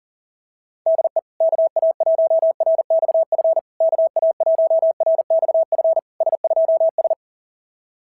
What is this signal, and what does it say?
Morse code, heard as a keyed tone, about 39 wpm, 690 Hz. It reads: DE KA1RXF KA1RXF S2S
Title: SOTA QSO Practice 40wpm